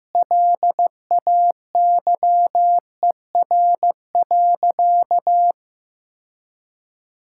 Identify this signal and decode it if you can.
Morse code, heard as a keyed tone, about 15 wpm, 700 Hz. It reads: LAYER.
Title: Sentences from Top 1600 Words 15wpm